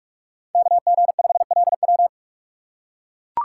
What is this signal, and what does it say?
Morse code, heard as a keyed tone, about 42 wpm, 700 Hz. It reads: KG5LW
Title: USA Call Signs - 5 Characters - 4x Character Spacing - 45wpm